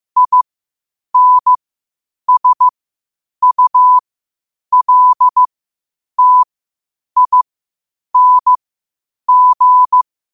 Morse code, heard as a keyed tone, about 15 wpm, spaced as for 8 wpm, 1000 Hz. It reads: INSULTING
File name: qotdfiles / 8wpm